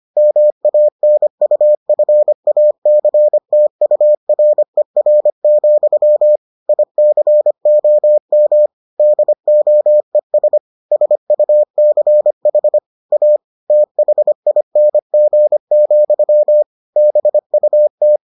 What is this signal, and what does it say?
Morse code, heard as a keyed tone, about 25 wpm, 600 Hz. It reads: MANUFACTURER, ICOM DOES SUCH A THING, BUT